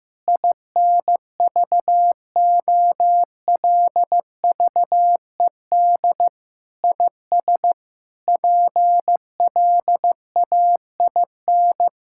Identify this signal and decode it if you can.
Morse code, heard as a keyed tone, about 16 wpm, 700 Hz. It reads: INVOLVED IS PLAIN
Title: Sentences from Top 1000 Words - Review Part 5 - 15wpm